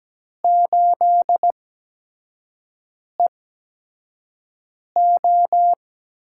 Morse code, heard as a keyed tone, about 17 wpm, spaced as for 5 wpm, 700 Hz. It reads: 8EO
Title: USA Call Signs - 4 Characters - 8x Character Spacing - 17wpm